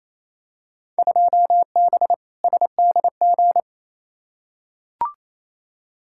Morse code, heard as a keyed tone, about 28 wpm, 700 Hz. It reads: 2B SDG
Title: Field Day Exchanges Encore - 28wpm